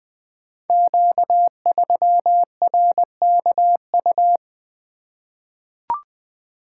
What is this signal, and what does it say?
Morse code, heard as a keyed tone, about 20 wpm, 700 Hz. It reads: Q3RKU
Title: Code Groups - 5 Letter:Numbers - 20wpm